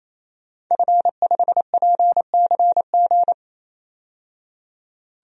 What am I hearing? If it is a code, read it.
Morse code, heard as a keyed tone, about 28 wpm, 700 Hz. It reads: F5PCG
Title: DXCC-100 Call Signs - 5 Characters - 28wpm